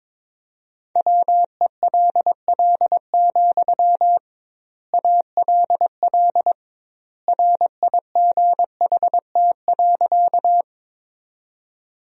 Morse code, heard as a keyed tone, about 22 wpm, 700 Hz. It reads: WELL, ALL RIGHT.
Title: 2x Word-Spacing - Sentences from Top 200 Words - 22wpm